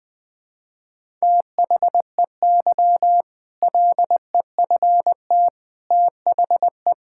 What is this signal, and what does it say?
Morse code, heard as a keyed tone, about 20 wpm, 700 Hz. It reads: THEY LEFT THE
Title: Sentences from Top 1000 Words - Review Part 5 - 20wpm